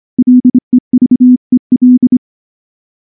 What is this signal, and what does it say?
Morse code, heard as a keyed tone, about 24 wpm, 250 Hz. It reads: LEVEL
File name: morse / morse-training-25WPM